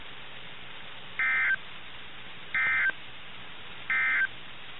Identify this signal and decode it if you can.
Emergency Alert System (SAME) tones, heard as harsh buzzing bursts. It is heard from 1.1 to 4.3 s.
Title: NOAA Weather Radio - KIH61